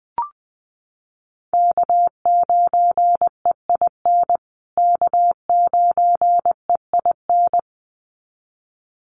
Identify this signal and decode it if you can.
Morse code, heard as a keyed tone, about 19 wpm, 700 Hz. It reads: K9EIN K9EIN
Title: USA Call Signs Repeated 2x 20wpm